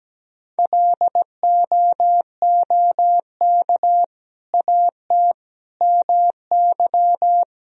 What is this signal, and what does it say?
Morse code, heard as a keyed tone, about 17 wpm, 700 Hz. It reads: LOOK AT MY
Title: Sentences from Top 1000 Words - Review Part 3 - 17wpm